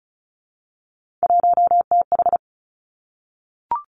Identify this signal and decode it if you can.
Morse code, heard as a keyed tone, about 36 wpm, 700 Hz. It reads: 1TH
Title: Three Challenging Letter-Number Combinations 35wpm